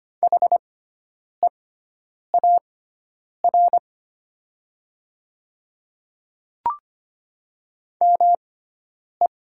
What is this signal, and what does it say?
Morse code, heard as a keyed tone, about 25 wpm, spaced as for 9 wpm, 700 Hz. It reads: HEAR ME
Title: 6x Character Spacing - Top 500 Words - 25wpm